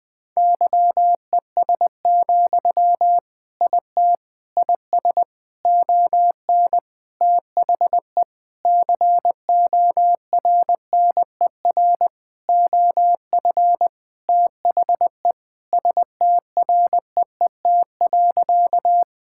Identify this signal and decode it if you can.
Morse code, heard as a keyed tone, about 20 wpm, 700 Hz. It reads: YES, IT IS ON THE CORNER OF THE STREET.
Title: Sentences from Top 1600 Words 20wpm